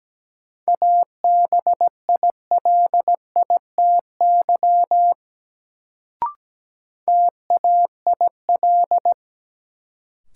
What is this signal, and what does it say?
Morse code, heard as a keyed tone, about 17 wpm, 700 Hz. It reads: ABILITY TAIL
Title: Sentences from Top 1000 Words - Review Part 3 - New Word Drill - 17wpm